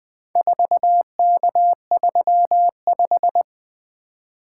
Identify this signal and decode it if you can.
Morse code, heard as a keyed tone, about 20 wpm, 700 Hz. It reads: 4K35
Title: Four Letter-Number Combinations - 20wpm